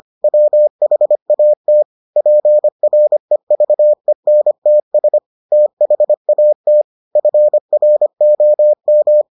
Morse code, heard as a keyed tone, about 25 wpm, 600 Hz. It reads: WHAT PREVENTS THAT FROM